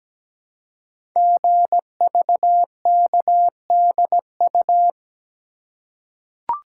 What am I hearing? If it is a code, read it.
Morse code, heard as a keyed tone, about 17 wpm, 700 Hz. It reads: GVKDU
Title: Code Groups - 5 Letter:Numbers - 17wpm